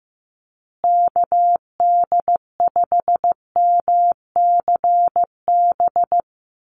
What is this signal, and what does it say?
Morse code, heard as a keyed tone, about 15 wpm, 700 Hz. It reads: KD5MCB
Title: USA Call Signs Repeated 3x 15wpm